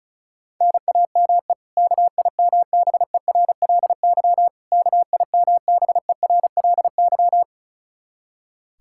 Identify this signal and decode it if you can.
Morse code, heard as a keyed tone, about 34 wpm, 700 Hz. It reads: NAME KIMBERLY KIMBERLY
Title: Practice QSO - Exchange 1 - Beginning - 35wpm